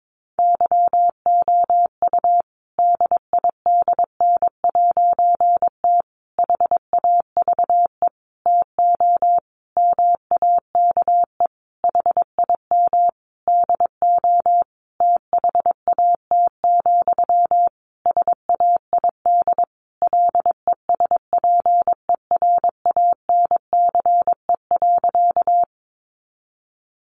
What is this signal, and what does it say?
Morse code, heard as a keyed tone, about 22 wpm, 700 Hz. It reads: YOU DIDN'T HAVE TO MAKE HIM DO THAT, SAID LESPERANCE.
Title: A Sound Of Thunder 22wpm